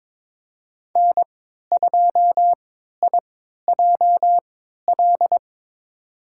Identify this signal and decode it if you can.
Morse code, heard as a keyed tone, about 22 wpm, spaced as for 13 wpm, 700 Hz. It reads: N2IJL
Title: USA Call Signs - 5 Characters - 3x Character Spacing - 22wpm